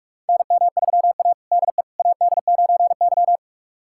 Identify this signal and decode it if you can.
Morse code, heard as a keyed tone, about 44 wpm, 690 Hz. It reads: NM3A DE AD9Y
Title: USA Callsign DE Callsign - 4 Characters - 45wpm